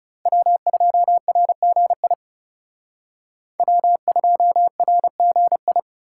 Morse code, heard as a keyed tone, about 35 wpm, 700 Hz. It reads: W2RGI W2RGI
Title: Speed Racing - USA 1x3 Call Signs 30wpm